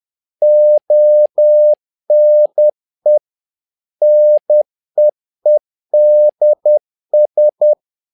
Morse code, heard as a keyed tone, about 10 wpm, 600 Hz. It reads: ONE NEEDS
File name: learning center / tower trouble / tower trouble 10wpm